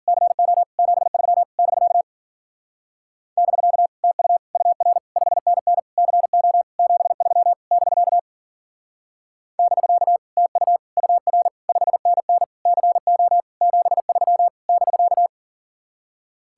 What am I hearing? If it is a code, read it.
Morse code, heard as a keyed tone, about 48 wpm, 700 Hz. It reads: CO 73 <BK> <BK> TU UR 5NN CO 73 <BK> <BK> TU UR 5NN CO 73 <BK>
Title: Speed-Racing Common QSO Phrases - 40wpm